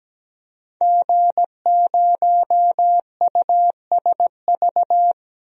Morse code, heard as a keyed tone, about 17 wpm, 700 Hz. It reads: G0USV
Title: DXCC-100 Call Signs - 5 Characters - 17wpm